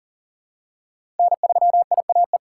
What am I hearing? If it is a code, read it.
Morse code, heard as a keyed tone, about 37 wpm, 700 Hz. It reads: N3IAE
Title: USA Call Signs - 2x Character Spacing - 40wpm